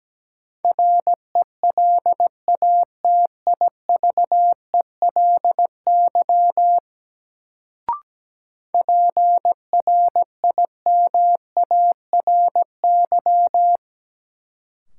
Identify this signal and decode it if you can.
Morse code, heard as a keyed tone, about 17 wpm, 700 Hz. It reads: RELATIVELY PRIMARY
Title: Sentences from Top 1100 Words - New Word Drill Alt - 17wpm